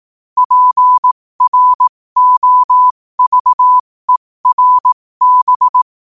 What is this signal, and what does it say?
Morse code, heard as a keyed tone, about 18 wpm, spaced as for 15 wpm, 1000 Hz. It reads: PROVERB
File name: qotdfiles / 15wpm